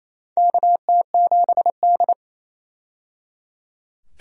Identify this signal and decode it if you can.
Morse code, heard as a keyed tone, about 28 wpm, 700 Hz. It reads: KT7D
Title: USA Call Signs - 4 Characters - 28wpm